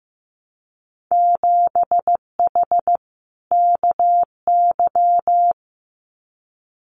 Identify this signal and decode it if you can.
Morse code, heard as a keyed tone, about 15 wpm, 700 Hz. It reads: 7H KY
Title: Winter Field Day Exchanges Encore 15wpm